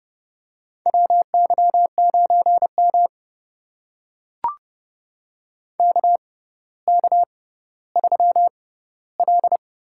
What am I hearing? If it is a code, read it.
Morse code, heard as a keyed tone, about 32 wpm, spaced as for 16 wpm, 700 Hz. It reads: WY9M KK3L
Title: USA Call Signs - 4 Characters - 6x Character Spacing - 30wpm